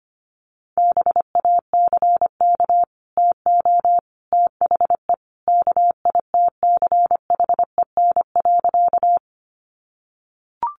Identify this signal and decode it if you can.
Morse code, heard as a keyed tone, about 25 wpm, 700 Hz. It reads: BACK TO THE KITCHEN.
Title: Sentences from Top 1400 Words 25wpm